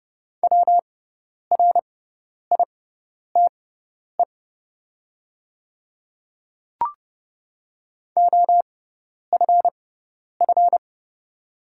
Morse code, heard as a keyed tone, about 29 wpm, spaced as for 10 wpm, 700 Hz. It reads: WRITE OFF
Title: 6x Character Spacing - Top 200 Words - 30wpm